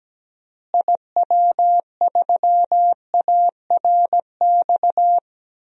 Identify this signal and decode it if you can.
Morse code, heard as a keyed tone, about 17 wpm, 700 Hz. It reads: IW3ARX
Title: DXCC-100 Call Signs - 6 Characters - Repeated 3x - 17wpm